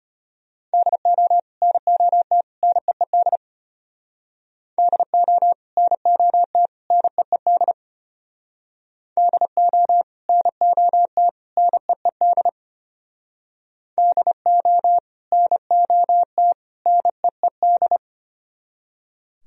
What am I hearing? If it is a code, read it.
Morse code, heard as a keyed tone, about 31 wpm, 700 Hz. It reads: DO NOT NEED DO NOT NEED DO NOT NEED DO NOT NEED
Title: Speed-Racing - Sets of 3 Words 25wpm